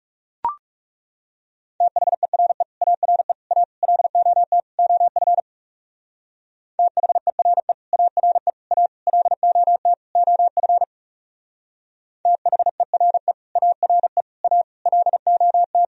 Morse code, heard as a keyed tone, about 36 wpm, 700 Hz. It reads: THERE ARE A LOT OF THERE ARE A LOT OF THERE ARE A LOT
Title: Speed-Racing - Sets of 5 Words 30wpm